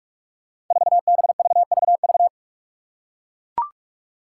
Morse code, heard as a keyed tone, about 41 wpm, 700 Hz. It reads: 4BVVV
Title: Five Challenging Letter-Number Combinations 45wpm